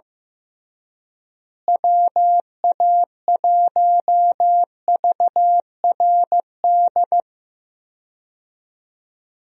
Morse code, heard as a keyed tone, about 15 wpm, 700 Hz. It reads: WA1VRD
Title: USA Call Signs - 6 Characters - Repeated 3x - 15wpm